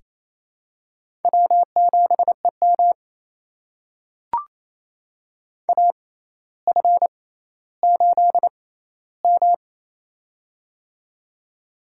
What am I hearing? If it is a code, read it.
Morse code, heard as a keyed tone, about 28 wpm, spaced as for 14 wpm, 700 Hz. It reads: W7EM AF8M
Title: USA Call Signs - 4 Characters - 6x Character Spacing - 28wpm